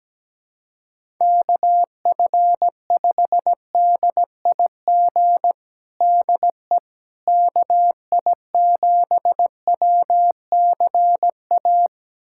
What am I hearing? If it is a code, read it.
Morse code, heard as a keyed tone, about 17 wpm, 700 Hz. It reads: KF5DIG DE KI7WCA
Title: USA Callsign DE Callsign 17wpm